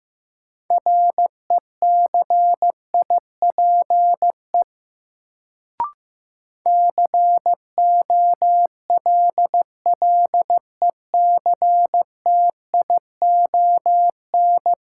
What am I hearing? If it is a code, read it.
Morse code, heard as a keyed tone, about 15 wpm, 700 Hz. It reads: RECIPE COLLECTION
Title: Sentences from Top 1200 Words - New Word Drill Alt - 15wpm